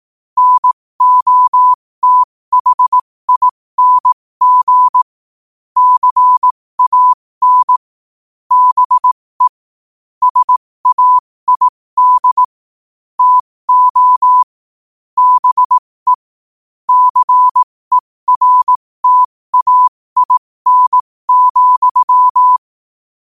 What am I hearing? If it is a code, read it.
Morse code, heard as a keyed tone, about 18 wpm, spaced as for 15 wpm, 1000 Hz. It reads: NOTHING CAN BE SAID TO BE CERTAIN,
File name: qotdfiles / 15wpm